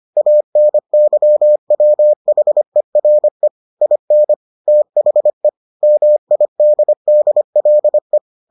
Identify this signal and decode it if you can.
Morse code, heard as a keyed tone, about 25 wpm, 600 Hz. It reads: ANYWHERE IN THE MIDDLE